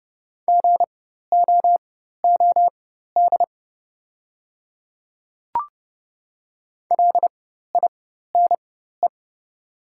Morse code, heard as a keyed tone, about 30 wpm, spaced as for 14 wpm, 700 Hz. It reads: GOOD LINE
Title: 4x Character Spacing - Top 200 Words - 30wpm